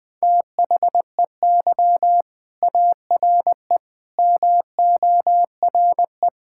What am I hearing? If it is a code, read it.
Morse code, heard as a keyed tone, about 20 wpm, 700 Hz. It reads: THEY ARE MORE
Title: Sentences with 5-Word Phrases - Spoken Beginning - 20wpm